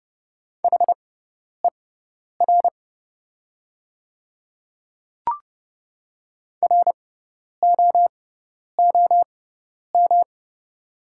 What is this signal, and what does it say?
Morse code, heard as a keyed tone, about 30 wpm, spaced as for 10 wpm, 700 Hz. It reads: HER ROOM
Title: 6x Character Spacing - Top 300 Words - 30wpm